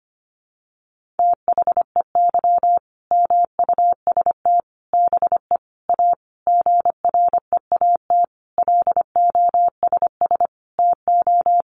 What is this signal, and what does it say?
Morse code, heard as a keyed tone, about 25 wpm, 700 Hz. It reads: THEY MUST BE A GREAT LOSS TO
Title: Sentences from Top 1400 Words 25wpm